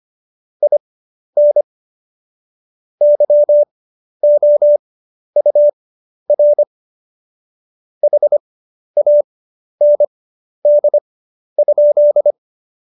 Morse code, heard as a keyed tone, about 25 wpm, spaced as for 11 wpm, 590 Hz. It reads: IN YOUR HAND?
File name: Jokes-25-char-10-eff